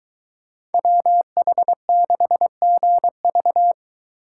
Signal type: Morse code

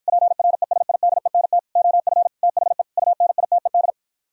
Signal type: Morse code

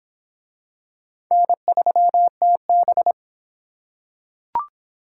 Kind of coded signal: Morse code

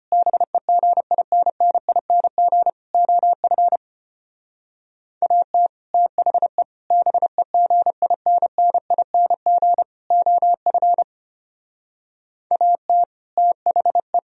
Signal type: Morse code

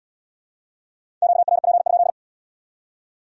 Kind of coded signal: Morse code

